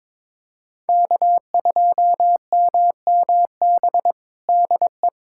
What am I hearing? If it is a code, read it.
Morse code, heard as a keyed tone, about 22 wpm, 700 Hz. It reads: K2MMB DE
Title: USA Callsign DE Callsign - 5 Characters - 22wpm